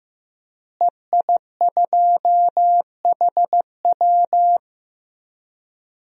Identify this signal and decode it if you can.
Morse code, heard as a keyed tone, about 15 wpm, 700 Hz. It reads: EI2HW
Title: DXCC-100 Call Signs - 5 Characters - Repeated 3x - 15wpm